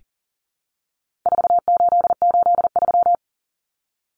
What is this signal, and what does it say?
Morse code, heard as a keyed tone, about 40 wpm, 700 Hz. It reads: 4883
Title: Numbers - 4-Digit 40wpm